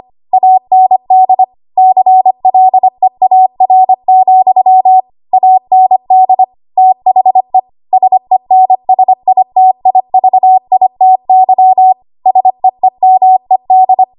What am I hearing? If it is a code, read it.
Morse code, heard as a keyed tone, about 25 wpm, 750 Hz. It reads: AND CLEAR, AND THE SENSITIVITY SEEMED
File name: Archive / 25 WPM / 25WPM